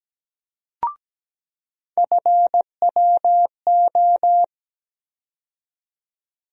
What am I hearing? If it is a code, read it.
Morse code, heard as a keyed tone, about 17 wpm, 700 Hz. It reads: FWO
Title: USA Call Sign Postfixes - 3 Letters 17wpm